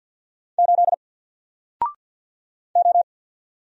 Morse code, heard as a keyed tone, about 48 wpm, 700 Hz. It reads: 8 O
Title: Single Character (Letter:Number:Punctuation:Prosign) Alt 50wpm